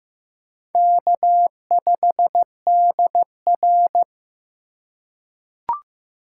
Morse code, heard as a keyed tone, about 15 wpm, 700 Hz. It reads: K5DR
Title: USA Call Signs - 4 Characters - 3x Character Spacing - 15wpm